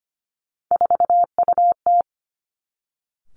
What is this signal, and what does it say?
Morse code, heard as a keyed tone, about 25 wpm, 700 Hz. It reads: 4UT
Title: Three Challenging Letter-Number Combinations 25wpm